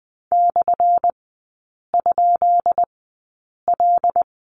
Morse code, heard as a keyed tone, about 20 wpm, spaced as for 8 wpm, 700 Hz. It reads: /?L
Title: 178 Lesson 29e - Slash - Three Characters 20wpm